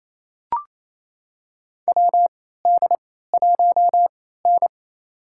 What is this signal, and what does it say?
Morse code, heard as a keyed tone, about 28 wpm, spaced as for 16 wpm, 700 Hz. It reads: WD1N
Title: USA Call Signs - 4 Characters - 3x Character Spacing - 28wpm